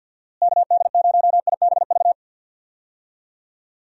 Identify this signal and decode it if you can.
Morse code, heard as a keyed tone, about 51 wpm, 700 Hz. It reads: KD0IBV
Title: USA Call Signs - 6 Characters - Repeated 2x - 50wpm